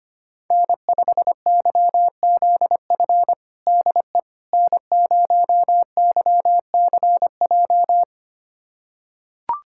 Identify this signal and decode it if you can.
Morse code, heard as a keyed tone, about 25 wpm, 700 Hz. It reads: N5YZF DE N0YCJ
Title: USA Callsign DE Callsign - 5 Characters - 25wpm